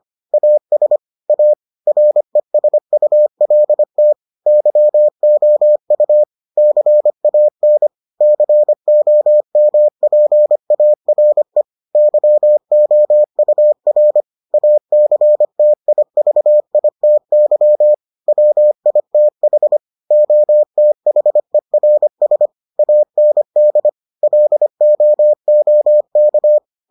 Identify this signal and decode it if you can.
Morse code, heard as a keyed tone, about 25 wpm, 600 Hz. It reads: AS A RESULT YOU CAN COMPARE YOUR ACTIVITY WITH OTHERS AND LOOK